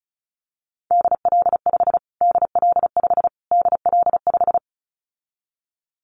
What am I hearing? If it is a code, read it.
Morse code, heard as a keyed tone, about 34 wpm, 700 Hz. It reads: DL5 DL5 DL5
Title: Callsign Prefixes for Top 25 DXCC - Repeated 3x - 35wpm